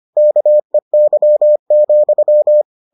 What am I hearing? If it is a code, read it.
Morse code, heard as a keyed tone, about 25 wpm, 600 Hz. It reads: KEY,